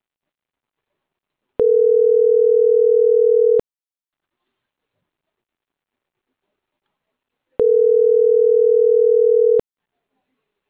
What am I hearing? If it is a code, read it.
Ringback tone (North America), 440 + 480 Hz.